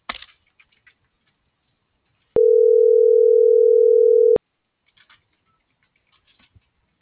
Ringback tone (North America), 440 + 480 Hz.